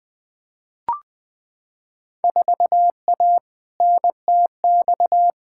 Morse code, heard as a keyed tone, about 20 wpm, 700 Hz. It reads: 4A NTX
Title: Field Day Exchanges - 20wpm